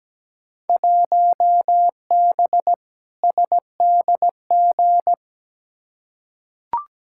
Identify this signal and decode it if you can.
Morse code, heard as a keyed tone, about 17 wpm, 700 Hz. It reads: 1B SDG
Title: Field Day Exchanges - 17wpm